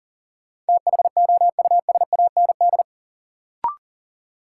Morse code, heard as a keyed tone, about 37 wpm, 700 Hz. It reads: THOUSAND
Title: Sentences from Top 600 Words - New Word Drill - 40wpm